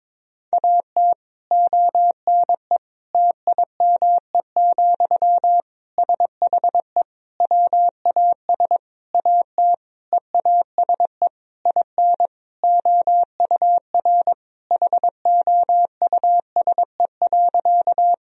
Morse code, heard as a keyed tone, about 22 wpm, 700 Hz. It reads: AT ONE TIME, SHE WAS AT EASE IN OUR HOUSE.
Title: Sentences from Top 1000 Words - Review Part 1 - 22wpm